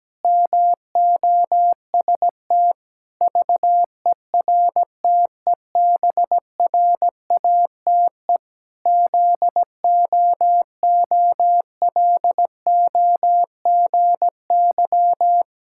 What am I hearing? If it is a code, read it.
Morse code, heard as a keyed tone, about 17 wpm, 700 Hz. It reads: MOST VERTEBRATE ZOOLOGY
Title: Pangram Challenge v2 - 17wpm